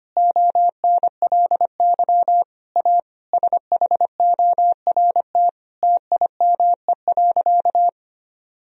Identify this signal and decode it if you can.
Morse code, heard as a keyed tone, about 25 wpm, 700 Hz. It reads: ONLY A SHORT TIME.
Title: Sentences from Top 1000 Words - Review Part 3 - 25wpm